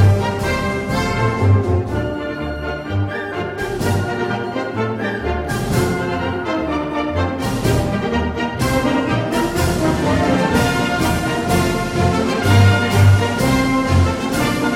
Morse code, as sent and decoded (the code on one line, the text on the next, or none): none